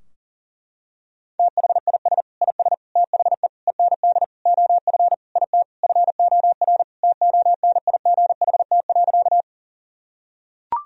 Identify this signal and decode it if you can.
Morse code, heard as a keyed tone, about 40 wpm, 700 Hz. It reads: THIS IS THE END OF IT FOR TONIGHT.
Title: Sentences from Top 1000 Words - Review Part 3 - 40wpm